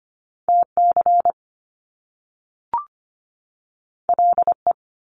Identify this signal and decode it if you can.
Morse code, heard as a keyed tone, about 25 wpm, 700 Hz. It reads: TC LE
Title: Top Two Letter Combinations in Words 25wpm